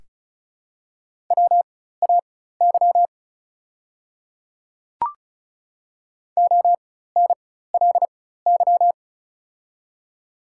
Morse code, heard as a keyed tone, about 36 wpm, spaced as for 16 wpm, 700 Hz. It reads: WAY ONLY
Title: 4x Character Spacing - Top 200 Words - 35wpm